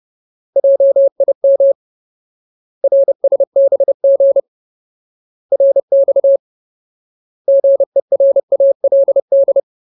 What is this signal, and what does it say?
Morse code, heard as a keyed tone, about 30 wpm, 550 Hz. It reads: JIM RSBG RX GERALD
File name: Ultimate Word List / 30WPM Ultimate